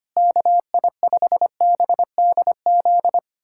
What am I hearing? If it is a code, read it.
Morse code, heard as a keyed tone, about 25 wpm, 700 Hz. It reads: KI5BDZ
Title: USA Callsign DE Callsign - 6 Characters - 25wpm